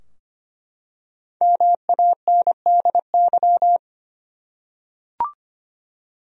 Morse code, heard as a keyed tone, about 25 wpm, 700 Hz. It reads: MANDY
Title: USA Common Names - 25wpm